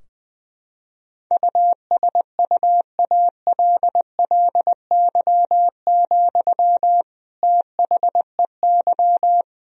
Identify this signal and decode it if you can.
Morse code, heard as a keyed tone, about 20 wpm, 700 Hz. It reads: USUALLY, THEY
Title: Sentences from Top 1000 Words - Review Part 3 - 20wpm